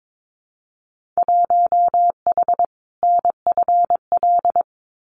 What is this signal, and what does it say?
Morse code, heard as a keyed tone, about 22 wpm, 700 Hz. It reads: 1H NFL
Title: Winter Field Day Exchanges 22wpm